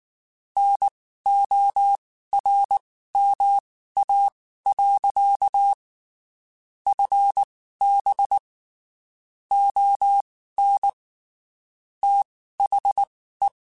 Morse code, heard as a keyed tone, about 19 wpm, spaced as for 14 wpm, 800 Hz. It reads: NORMA. FB ON THE
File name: (spaced) / QSOs Spaced-mp3s / WD0FBT 1 @18 sp